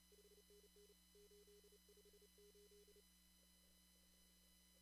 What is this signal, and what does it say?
Morse code, heard as a keyed tone, about 31 wpm, 410 Hz. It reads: 5NN 858
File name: CW / kt5j wpx cw-op-k5pi